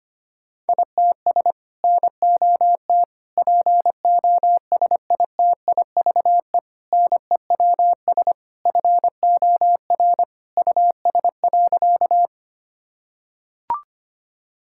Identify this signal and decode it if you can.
Morse code, heard as a keyed tone, about 25 wpm, 700 Hz. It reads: ITS NOT POSITIVE NEWS FOR US.
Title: Sentences from Top 1000 Words - Review Part 5 - 25wpm